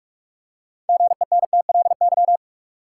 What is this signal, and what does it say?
Morse code, heard as a keyed tone, about 46 wpm, 700 Hz. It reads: GENTLY
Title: Six Letter Words - 45wpm